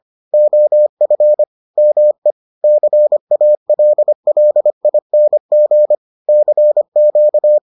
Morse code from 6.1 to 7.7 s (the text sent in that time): CQ